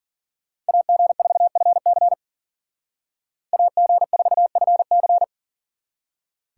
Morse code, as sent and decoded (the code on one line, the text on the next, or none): .- --. ....- ..-. -.-. / .- --. ....- ..-. -.-.
AG4FC AG4FC